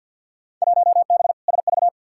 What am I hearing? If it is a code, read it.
Morse code, heard as a keyed tone, about 51 wpm, 700 Hz. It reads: 1B SV
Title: Field Day Exchanges - 50wpm